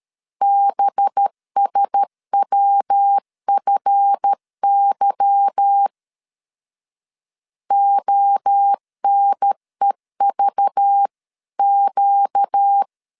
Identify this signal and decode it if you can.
Morse code, heard as a keyed tone, about 13 wpm, 800 Hz. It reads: BSWFY ONEVQ
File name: CW / DET / 1 9 1500C S B 12wpm 3819